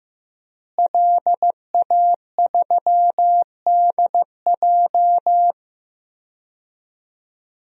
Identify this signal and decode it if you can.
Morse code, heard as a keyed tone, about 15 wpm, 700 Hz. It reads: LA3DJ
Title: DXCC-100 Call Signs - 5 Characters - Repeated 2x - 15wpm